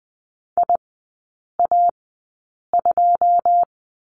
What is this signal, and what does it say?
Morse code, heard as a keyed tone, about 20 wpm, spaced as for 8 wpm, 690 Hz. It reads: IA2
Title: 145 Lesson 24e - Letter y - Three Characters 20wpm